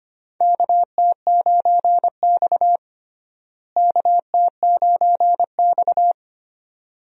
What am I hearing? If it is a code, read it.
Morse code, heard as a keyed tone, about 25 wpm, 700 Hz. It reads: KT9X KT9X
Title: USA Call Signs - 4 Characters - Repeated 2x - 25wpm